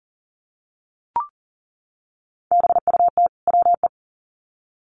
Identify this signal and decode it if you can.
Morse code, heard as a keyed tone, about 40 wpm, 700 Hz. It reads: BUT WE
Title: Sets of 2 Words 40wpm